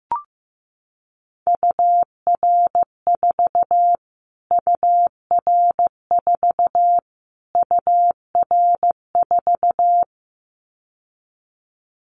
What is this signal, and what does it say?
Morse code, heard as a keyed tone, about 15 wpm, 700 Hz. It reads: UR4 UR4 UR4
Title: Callsign Prefixes for Top 25 DXCC - Repeated 3x - 15wpm